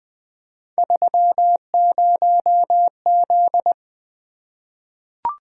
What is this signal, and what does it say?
Morse code, heard as a keyed tone, about 20 wpm, 700 Hz. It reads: 30Z
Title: Three Letter-Number Combinations - 20wpm